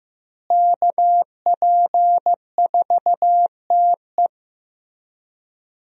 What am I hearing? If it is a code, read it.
Morse code, heard as a keyed tone, about 16 wpm, 700 Hz. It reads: KP4TE
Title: DXCC-100 Call Signs - 5 Characters - Repeated 2x - 15wpm